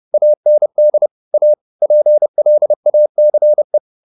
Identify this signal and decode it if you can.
Morse code, heard as a keyed tone, about 30 wpm, 600 Hz. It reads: AND A PLACE
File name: the old man and the sea / the old man and the sea 1 30wpm